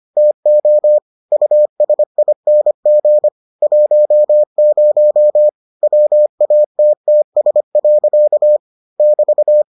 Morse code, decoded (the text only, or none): TO USING 10 WATTS. <BT>